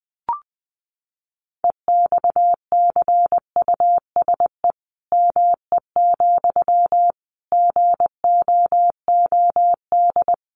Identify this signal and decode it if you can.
Morse code, heard as a keyed tone, about 20 wpm, 700 Hz. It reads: EXCUSE ME, GOOD
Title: Sentences from Top 700 Words 20wpm